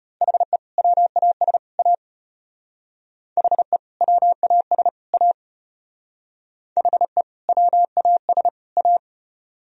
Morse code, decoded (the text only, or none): HE WAS A HE WAS A HE WAS A